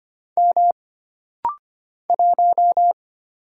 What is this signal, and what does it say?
Morse code, heard as a keyed tone, about 25 wpm, 700 Hz. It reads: M 1
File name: Single Character (Letter:Number:Punctuation:Prosign) - Rapid-Fire - 25wpm